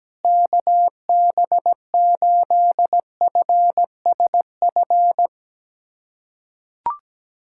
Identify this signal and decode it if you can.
Morse code, heard as a keyed tone, about 17 wpm, 700 Hz. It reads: KB8FSF
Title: USA Callsign DE Callsign 17wpm